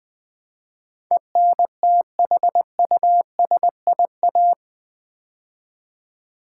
Morse code, heard as a keyed tone, about 20 wpm, 700 Hz. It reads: ENTHUSIA
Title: Build A Word - 4 Letters At A Time - 10-Letter Words - 20wpm